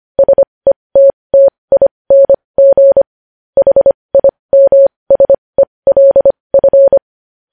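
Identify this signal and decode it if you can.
Morse code, heard as a keyed tone, about 25 wpm, spaced as for 20 wpm, 550 Hz. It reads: SETTING HIMSELF
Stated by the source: Morse code at 20 wpm.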